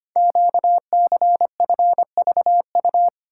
Morse code, heard as a keyed tone, about 25 wpm, 700 Hz. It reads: QCFVU
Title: Code Groups - 5 Letters - 25wpm